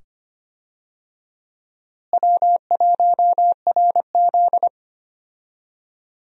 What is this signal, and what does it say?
Morse code, heard as a keyed tone, about 25 wpm, 700 Hz. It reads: W1RZ
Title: USA Call Signs - 4 Characters - Repeated 2x - 25wpm